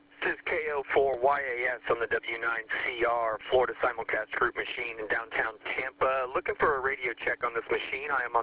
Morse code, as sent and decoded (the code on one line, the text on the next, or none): none